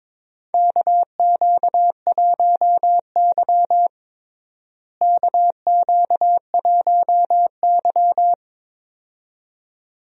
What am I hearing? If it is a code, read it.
Morse code, heard as a keyed tone, about 22 wpm, 700 Hz. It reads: KQ1Y KQ1Y
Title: USA Call Signs - 4 Characters - Repeated 2x - 22wpm